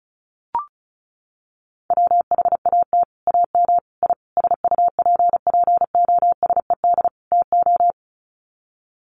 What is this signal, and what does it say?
Morse code, heard as a keyed tone, about 34 wpm, 700 Hz. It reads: WHAT AM I SUPPOSED TO
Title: Sets of 5 Words Encore2 35wpm